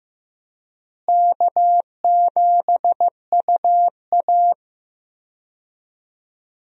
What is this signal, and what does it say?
Morse code, heard as a keyed tone, about 15 wpm, 700 Hz. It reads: K7UA
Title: USA Call Signs - 4 Characters - Repeated 2x - 15wpm